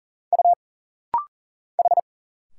Morse code, heard as a keyed tone, about 40 wpm, 700 Hz. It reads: U H
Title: Single Letter-Number - Rapid-Fire 40wpm